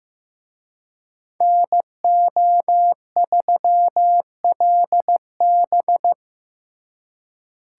Morse code, heard as a keyed tone, about 18 wpm, 700 Hz. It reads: NO3LB
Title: Speed Racing - USA Call Signs 15wpm